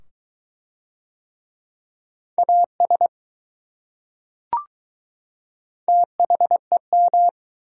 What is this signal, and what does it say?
Morse code, heard as a keyed tone, about 20 wpm, 700 Hz. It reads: AS THEM